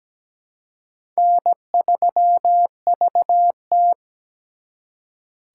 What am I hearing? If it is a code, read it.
Morse code, heard as a keyed tone, about 20 wpm, 700 Hz. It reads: N3VT
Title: Speed Racing - USA 1x2 Call Signs 17wpm